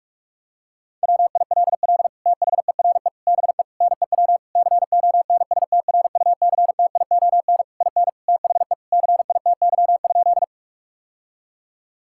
Morse code, heard as a keyed tone, about 41 wpm, 700 Hz. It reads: WILL THERE BE NEW CONSTRUCTION IN THE CITY?
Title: Sentences from Top 1600 Words 45wpm